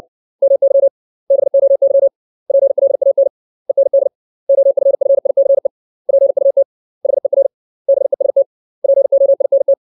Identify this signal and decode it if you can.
Morse code, heard as a keyed tone, about 60 wpm, 550 Hz. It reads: MY BOY PLAN END OFFICE PUT HER BEST POINT
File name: 632 words / 632 Words 60WPM